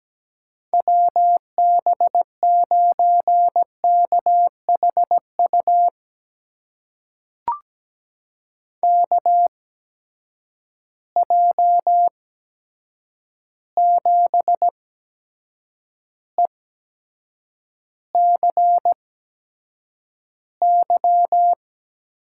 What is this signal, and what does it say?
Morse code, heard as a keyed tone, about 17 wpm, spaced as for 7 wpm, 700 Hz. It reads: WB9KHU KJ7ECY
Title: USA Call Signs - 6 Characters - 8x Character Spacing - 17wpm